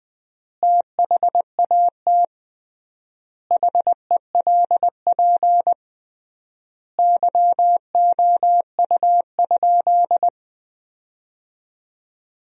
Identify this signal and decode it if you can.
Morse code, heard as a keyed tone, about 20 wpm, 700 Hz. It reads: THAT HELP YOU?
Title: 3x Word-Spacing - Sentences from Top 200 Words - 20wpm